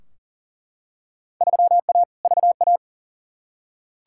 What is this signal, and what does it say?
Morse code, heard as a keyed tone, about 40 wpm, 700 Hz. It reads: 3A VA
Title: Field Day Exchanges Encore - 40wpm